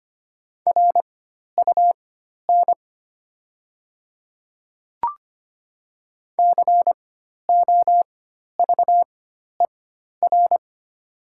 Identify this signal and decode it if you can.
Morse code, heard as a keyed tone, about 25 wpm, spaced as for 12 wpm, 690 Hz. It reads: RUN COVER
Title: 4x Character Spacing - Top 300 Words - 25wpm